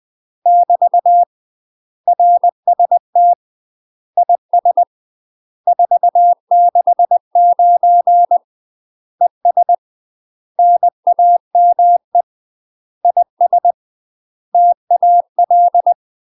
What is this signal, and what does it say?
Morse code, heard as a keyed tone, about 20 wpm, 700 Hz. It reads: <BT> RST IS 469 ES NAME IS TAL
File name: UK MASTER QSOs TRAINING / 20WPM / UK MASTER QSOs 0042